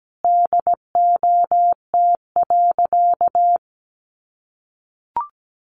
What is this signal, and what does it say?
Morse code, heard as a keyed tone, about 17 wpm, 700 Hz. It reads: DOT.
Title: Sentences from Top 1500 Words 17wpm